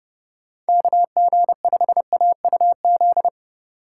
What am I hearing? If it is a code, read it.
Morse code, heard as a keyed tone, about 35 wpm, 700 Hz. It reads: KG5AUZ